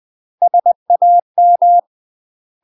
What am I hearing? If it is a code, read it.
Morse code, heard as a keyed tone, about 20 wpm, 700 Hz. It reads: SAM
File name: UK MASTER QSOs TRAINING / 20WPM / UK MASTER QSOs 0065